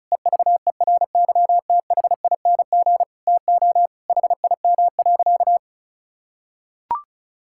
Morse code, heard as a keyed tone, about 36 wpm, 700 Hz. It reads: EVERYTHING TO HIM.
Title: Sentences from Top 1600 Words 35wpm